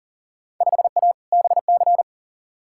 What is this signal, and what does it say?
Morse code, heard as a keyed tone, about 40 wpm, 700 Hz. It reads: 5A BC
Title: Field Day Exchanges - 40wpm